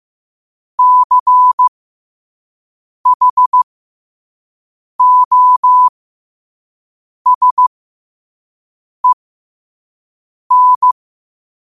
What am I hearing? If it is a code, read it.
Morse code, heard as a keyed tone, about 15 wpm, spaced as for 5 wpm, 1000 Hz. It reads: CHOSEN
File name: qotdfiles / 5wpm